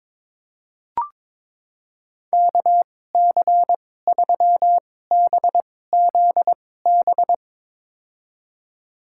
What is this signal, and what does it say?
Morse code, heard as a keyed tone, about 22 wpm, spaced as for 16 wpm, 700 Hz. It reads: KC3BZB
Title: USA Call Signs - 6 Characters - 2x Character Spacing - 22wpm